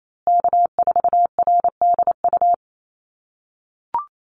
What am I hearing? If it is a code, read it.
Morse code, heard as a keyed tone, about 28 wpm, 700 Hz. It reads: K4RDU
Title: USA Call Signs Repeated 3x 28wpm